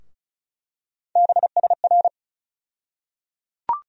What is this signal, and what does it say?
Morse code, heard as a keyed tone, about 36 wpm, 700 Hz. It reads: BSR
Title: USA Call Sign Postfixes - 3 Letters 35wpm